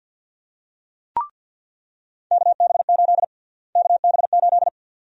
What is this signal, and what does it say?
Morse code, heard as a keyed tone, about 53 wpm, 700 Hz. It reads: KB8 KB8
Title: USA Call Sign Prefixes - Repeated 2x - 50wpm